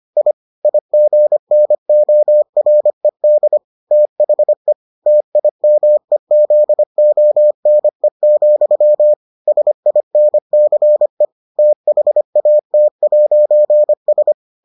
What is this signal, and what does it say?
Morse code, heard as a keyed tone, about 25 wpm, 600 Hz. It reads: I IGNORED THE TIMEZONE, SINCE THAT'S